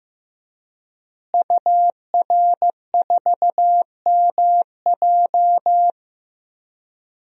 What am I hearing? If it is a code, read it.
Morse code, heard as a keyed tone, about 15 wpm, 700 Hz. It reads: UR4MJ
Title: DXCC-100 Call Signs - 5 Characters - Repeated 2x - 15wpm